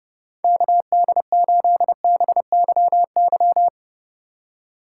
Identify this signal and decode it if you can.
Morse code, heard as a keyed tone, about 30 wpm, 700 Hz. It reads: KD8BYY